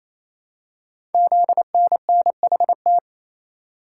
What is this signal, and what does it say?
Morse code, heard as a keyed tone, about 28 wpm, 700 Hz. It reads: ZNNHT
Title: Code Groups - 5 Letters - 28wpm